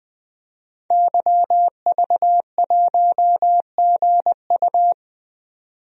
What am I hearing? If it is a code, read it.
Morse code, heard as a keyed tone, about 20 wpm, 700 Hz. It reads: YV1GU
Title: DXCC-100 Call Signs - 5 Characters - 20wpm